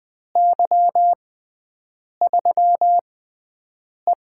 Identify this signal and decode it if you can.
Morse code, heard as a keyed tone, about 20 wpm, spaced as for 7 wpm, 700 Hz. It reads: Y3E
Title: USA Call Signs - 4 Characters - 6x Character Spacing - 20wpm